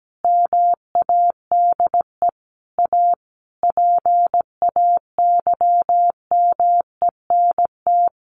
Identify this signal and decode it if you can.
Morse code, heard as a keyed tone, about 17 wpm, 700 Hz. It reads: MADE A PAYMENT
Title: Sentences from Top 1400 Words 17wpm